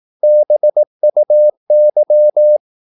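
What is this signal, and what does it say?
Morse code, heard as a keyed tone, about 18 wpm, 600 Hz. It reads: BUY
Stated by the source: Morse practice at 18 wpm, English text, from CW med Gnister.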